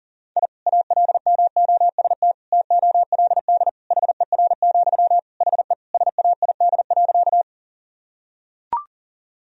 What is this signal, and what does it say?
Morse code, heard as a keyed tone, about 37 wpm, 700 Hz. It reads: I ALMOST TOLD HER, HE SAID.
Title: Sentences from Top 1000 Words - Review Part 4 - 40wpm